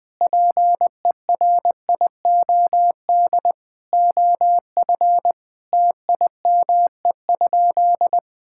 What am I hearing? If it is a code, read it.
Morse code, heard as a keyed tone, about 20 wpm, 700 Hz. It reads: PERIOD OF TIME?
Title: Sentences from Top 1000 Words - Review Part 5 - 20wpm